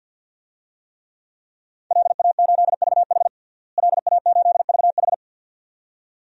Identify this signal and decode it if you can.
Morse code, heard as a keyed tone, about 52 wpm, 700 Hz. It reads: LA8VH LA8VH
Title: DXCC-100 Call Signs - 5 Characters - Repeated 2x - 50wpm